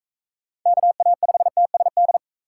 Morse code, heard as a keyed tone, about 42 wpm, 700 Hz. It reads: KA5TSD